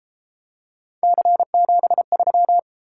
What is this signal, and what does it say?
Morse code, heard as a keyed tone, about 30 wpm, 700 Hz. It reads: C73